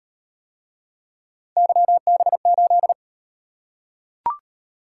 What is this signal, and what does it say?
Morse code, heard as a keyed tone, about 34 wpm, 700 Hz. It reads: YB8